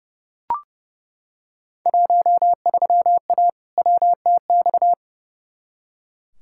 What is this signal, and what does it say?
Morse code, heard as a keyed tone, about 30 wpm, 700 Hz. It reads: 13A WTX
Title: Field Day Exchanges Encore - 30wpm